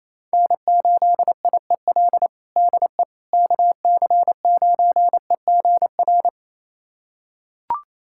Morse code, heard as a keyed tone, about 28 wpm, 700 Hz. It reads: N8IEL DE KC9EGR
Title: USA Callsign DE Callsign 28wpm